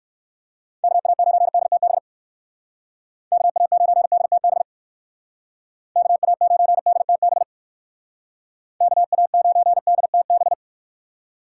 Morse code, heard as a keyed tone, about 56 wpm, 700 Hz. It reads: KA9DTB KA9DTB KA9DTB KA9DTB
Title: Speed Racing - USA 2x3 Call Signs 45wpm